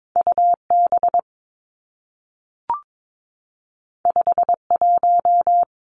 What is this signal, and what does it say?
Morse code, heard as a keyed tone, about 22 wpm, 700 Hz. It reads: U B 5 1
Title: Two Challenging Letter-Number Combinations 22wpm